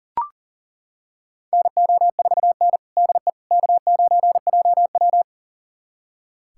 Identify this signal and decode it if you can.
Morse code, heard as a keyed tone, about 40 wpm, 700 Hz. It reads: NO4N DE K9JW
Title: USA Callsign DE Callsign - 4 Characters - 40wpm